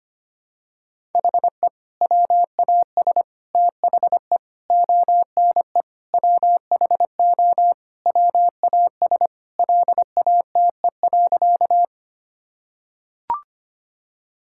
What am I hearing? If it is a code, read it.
Morse code, heard as a keyed tone, about 25 wpm, 700 Hz. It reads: HE WAS THE ONE WHO WAS LATE.
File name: practice / 25wpm / Sentences with 2-Word Phrases - 25wpm